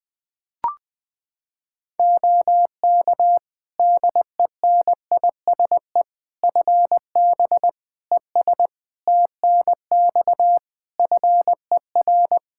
Morse code, heard as a keyed tone, about 21 wpm, 700 Hz. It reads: OK DENISE FB ES TNX FER
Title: Practice QSO - Exchange 2 - Info - 20wpm